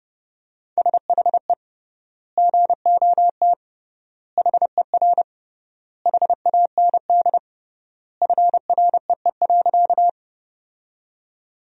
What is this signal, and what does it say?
Morse code, heard as a keyed tone, about 30 wpm, 700 Hz. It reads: SHE GOT HER HAND FREE.
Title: 3x Word-Spacing - Sentences from Top 500 Words - 30wpm